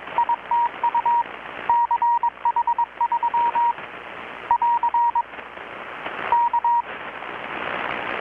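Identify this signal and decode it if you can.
Morse code, heard as a keyed tone, about 21 wpm, 950 Hz. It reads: ITU CH3 <AR> K
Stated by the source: Morse from KPH's Night of Nights XX maritime radiotelegraphy event, received in Chiapas, Mexico about 3,800 km away on a Kenwood R-600 with a grounded 10 m long-wire antenna.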